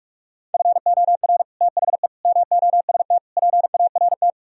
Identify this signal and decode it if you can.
Morse code, heard as a keyed tone, about 49 wpm, 700 Hz. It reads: FOR THE MOST PART